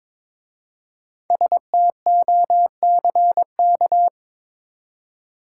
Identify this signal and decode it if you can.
Morse code, heard as a keyed tone, about 22 wpm, 690 Hz. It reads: STOCK
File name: 5 Letter Words - First 3 Letters Given - 22wpm